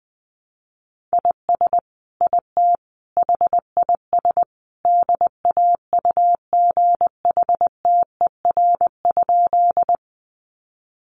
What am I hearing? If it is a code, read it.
Morse code, heard as a keyed tone, about 20 wpm, 700 Hz. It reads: IS IT HIS DAUGHTER?
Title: Sentences from Top 1500 Words 20wpm